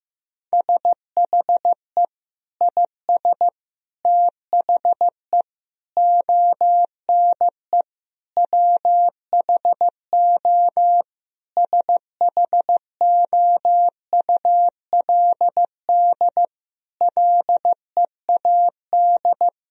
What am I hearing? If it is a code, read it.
Morse code, heard as a keyed tone, about 15 wpm, 700 Hz. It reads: SHE IS THE ONE WHO SHOULD LEAD
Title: Sentences from Top 1000 Words - Review Part 2 - 15wpm